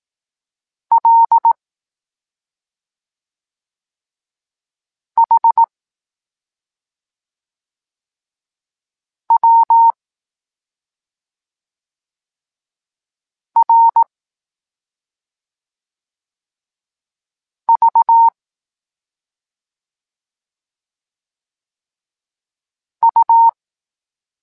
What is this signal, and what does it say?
Morse code, heard as a keyed tone, about 18 wpm, 920 Hz. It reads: L H W R V U
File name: Tekst-26 1